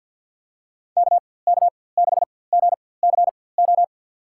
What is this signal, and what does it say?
Morse code, heard as a keyed tone, about 47 wpm, spaced as for 23 wpm, 700 Hz. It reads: KK6GCO
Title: USA Call Signs - 6 Characters - 4x Character Spacing - 50wpm